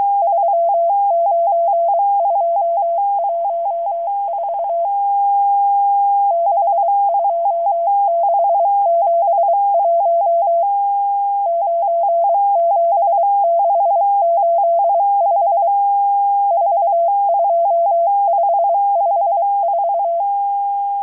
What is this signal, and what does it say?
Morse code, heard as a keyed tone, about 23 wpm, 680 Hz. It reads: 39214 62671 97685 42554
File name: HM02- -0437utc-7351khz